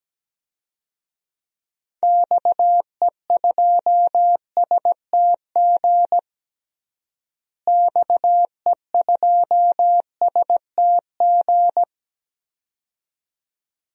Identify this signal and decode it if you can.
Morse code, heard as a keyed tone, about 17 wpm, 700 Hz. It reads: XE2STG XE2STG
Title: DXCC-100 Call Signs - 6 Characters - Repeated 2x - 17wpm